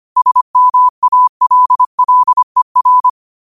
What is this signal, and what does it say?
Morse code, heard as a keyed tone, about 25 wpm, 1000 Hz. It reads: SMALLER
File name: 25wpm